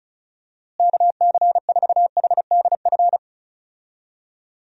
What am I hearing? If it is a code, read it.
Morse code, heard as a keyed tone, about 36 wpm, 700 Hz. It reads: KC4HDF
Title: USA Call Signs - 6 Characters - Repeated 3x - 35wpm